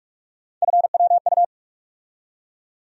Morse code, heard as a keyed tone, about 42 wpm, 700 Hz. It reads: FWU
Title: USA Call Sign Postfixes - 3 Letters 45wpm